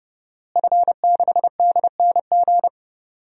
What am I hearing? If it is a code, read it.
Morse code, heard as a keyed tone, about 32 wpm, 700 Hz. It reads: F6DNG
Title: DXCC-100 Call Signs - 5 Characters - Repeated 3x - 30wpm